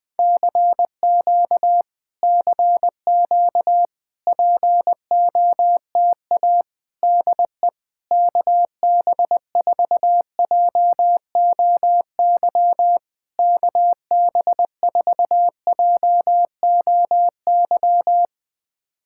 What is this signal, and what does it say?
Morse code, heard as a keyed tone, about 20 wpm, 690 Hz. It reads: CQ CQ POTA DE KB4JOY KB4JOY
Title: POTA QSO Practice - 20wpm